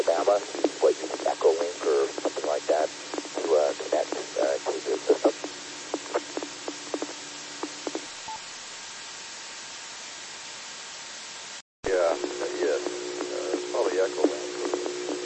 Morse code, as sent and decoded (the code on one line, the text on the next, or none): none